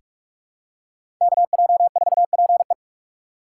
Morse code, heard as a keyed tone, about 42 wpm, 700 Hz. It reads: KJ4PE
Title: USA Call Signs - 5 Characters - 4x Character Spacing - 45wpm